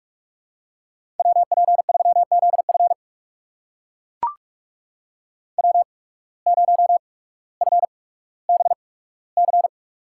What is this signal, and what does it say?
Morse code, heard as a keyed tone, about 43 wpm, spaced as for 19 wpm, 700 Hz. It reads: WP3ZF W0FBC
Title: USA Call Signs - 5 Characters - 8x Character Spacing - 45wpm